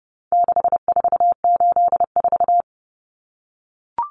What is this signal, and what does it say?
Morse code, heard as a keyed tone, about 30 wpm, 700 Hz. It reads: 6484
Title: Numbers - 4-Digit 30wpm